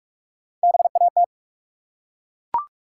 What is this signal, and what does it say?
Morse code, heard as a keyed tone, about 42 wpm, 700 Hz. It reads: BAT
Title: Three Letter Words - 45wpm